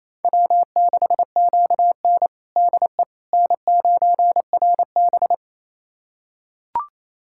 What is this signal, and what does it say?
Morse code, heard as a keyed tone, about 27 wpm, 700 Hz. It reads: W6QN DE N9RB
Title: USA Callsign DE Callsign - 4 Characters - 28wpm